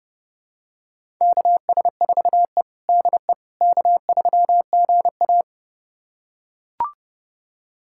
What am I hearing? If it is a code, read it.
Morse code, heard as a keyed tone, about 30 wpm, 700 Hz. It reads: KS4E DE K3GA
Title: USA Callsign DE Callsign - 4 Characters - 30wpm